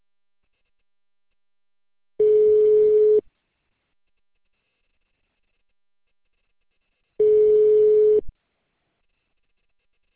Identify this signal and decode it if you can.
Ringback tone (Europe, 425 Hz).